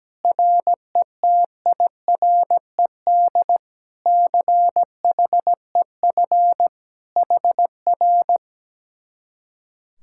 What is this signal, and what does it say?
Morse code, heard as a keyed tone, about 17 wpm, 700 Hz. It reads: RETIRED CHEF HR
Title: Speed-Racing Common QSO Phrases - 17wpm